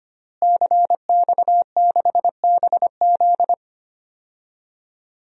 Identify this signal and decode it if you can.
Morse code, heard as a keyed tone, about 25 wpm, 700 Hz. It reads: CX6BZ
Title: DXCC-100 Call Signs - 5 Characters - Repeated 2x - 25wpm